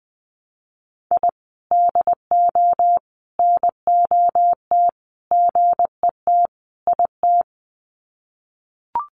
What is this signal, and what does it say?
Morse code, heard as a keyed tone, about 20 wpm, 700 Hz. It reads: I DO NOT GET IT
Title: Sets of 5 Words Encore 20wpm